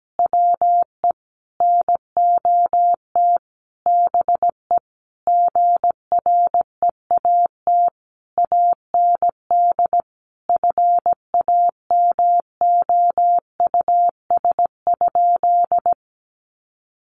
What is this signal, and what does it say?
Morse code, heard as a keyed tone, about 17 wpm, 700 Hz. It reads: WE NOT BE GREAT AND FAMOUS?
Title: Sentences from Top 1500 Words 17wpm